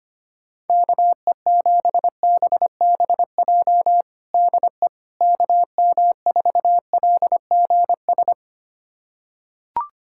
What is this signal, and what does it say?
Morse code, heard as a keyed tone, about 25 wpm, 700 Hz. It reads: KE7BBJ DE KM4LGS
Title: USA Callsign DE Callsign 25wpm